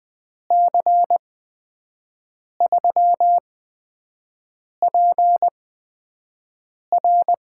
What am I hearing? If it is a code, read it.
Morse code, heard as a keyed tone, about 20 wpm, spaced as for 5 wpm, 700 Hz. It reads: C3PR
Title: USA Call Signs - 5 Characters - 8x Character Spacing - 20wpm